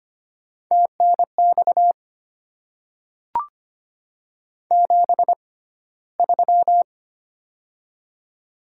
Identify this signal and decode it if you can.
Morse code, heard as a keyed tone, about 25 wpm, spaced as for 15 wpm, 700 Hz. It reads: TNX 73
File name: Top 20 QSO Elements - 6x Character Spacing - 25wpm